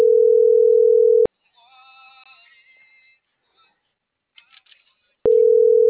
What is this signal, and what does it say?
Ringback tone (North America), 440 + 480 Hz.